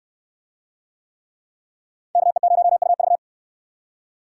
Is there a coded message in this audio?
Yes: Morse code from 2.2 to 3.2 s, heard as a keyed tone, about 64 wpm, 690 Hz. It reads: KE0UV